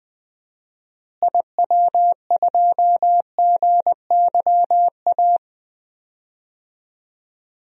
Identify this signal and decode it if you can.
Morse code, heard as a keyed tone, about 20 wpm, 700 Hz. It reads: IW2GYA
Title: DXCC-100 Call Signs - 6 Characters - Repeated 2x - 20wpm